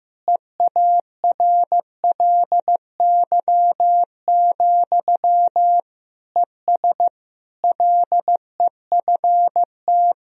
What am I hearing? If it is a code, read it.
Morse code, heard as a keyed tone, about 16 wpm, 700 Hz. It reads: EARLY, ES LEFT
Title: Sentences from Top 400 Words and Abbreviations 15wpm